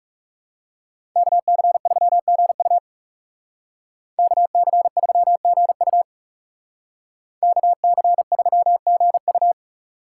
Morse code, heard as a keyed tone, about 36 wpm, 700 Hz. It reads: KC3GU KC3GU KC3GU
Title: Speed Racing - USA 2x2 Call Signs 30wpm